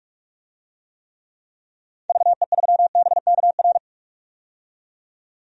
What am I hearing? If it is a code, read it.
Morse code, heard as a keyed tone, about 42 wpm, 700 Hz. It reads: VE3BKR
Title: DXCC-100 Call Signs - 6 Characters - 45wpm